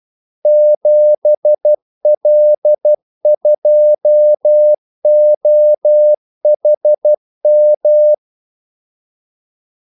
Morse code, heard as a keyed tone, about 12 wpm, 600 Hz. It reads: 7L2OHM
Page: 2 ekstra ordmellemrum mellem kaldesignalerne